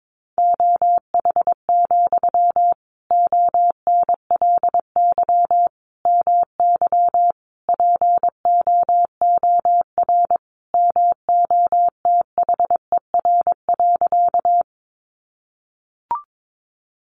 Morse code, heard as a keyed tone, about 22 wpm, 700 Hz. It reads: OH, ONLY MY POOR MOTHER.
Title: Sentences from Top 1000 Words 22wpm